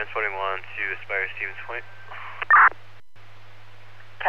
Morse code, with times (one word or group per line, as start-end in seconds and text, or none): none